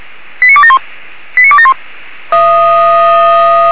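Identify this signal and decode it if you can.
Five-tone selective call (ZVEI), heard as quick runs of five tones, tones of about 70 ms. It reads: 89271 89271
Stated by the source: ZVEI five-tone call 89273 received on SDR.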